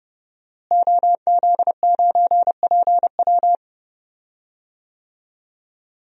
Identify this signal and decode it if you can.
Morse code, heard as a keyed tone, about 30 wpm, 700 Hz. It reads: OZ9PW
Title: DXCC-100 Call Signs - 5 Characters - Repeated 2x - 30wpm